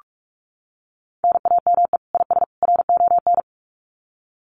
Morse code, heard as a keyed tone, about 44 wpm, 700 Hz. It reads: NAME IS RON
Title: Common QSO Phrases 45wpm